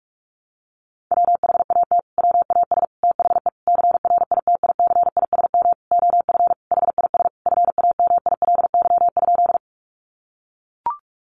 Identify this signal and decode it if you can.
Morse code, heard as a keyed tone, about 44 wpm, 700 Hz. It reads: WHAT WAS THE CRITICISM OF HIS FAMILY?
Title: Sentences from Top 1300 Words 45wpm